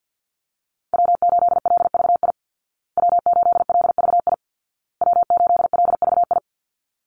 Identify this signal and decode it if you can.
Morse code, heard as a keyed tone, about 50 wpm, 700 Hz. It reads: W8LVI W8LVI W8LVI
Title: Call Signs Repeated 2x - 50wpm